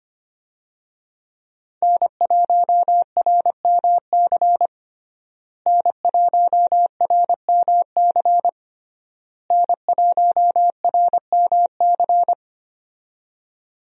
Morse code, heard as a keyed tone, about 25 wpm, 700 Hz. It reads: N1RMC N1RMC N1RMC
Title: USA Call Signs - 5 Characters - Repeated 3x - 25wpm